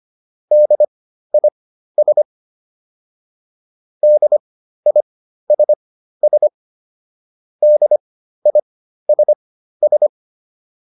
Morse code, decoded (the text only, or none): DIS DISS DISS